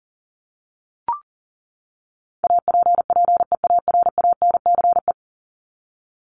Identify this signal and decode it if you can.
Morse code, heard as a keyed tone, about 40 wpm, 700 Hz. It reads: APPEARANCE
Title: Top 1001-1250 Words 40wpm